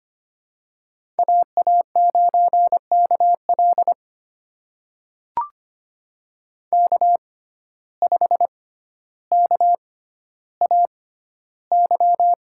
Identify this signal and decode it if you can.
Morse code, heard as a keyed tone, about 25 wpm, spaced as for 13 wpm, 700 Hz. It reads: AA9KL K5KAY
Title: USA Call Signs - 5 Characters - 6x Character Spacing - 25wpm